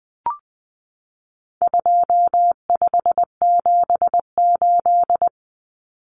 Morse code, heard as a keyed tone, about 20 wpm, 700 Hz. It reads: 2578
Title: Common Cut Numbers - 4-Digit 20wpm